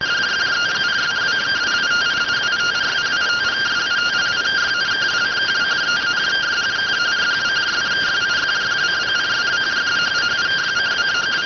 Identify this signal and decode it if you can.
Two-tone teleprinter signal, about 170 Hz shift.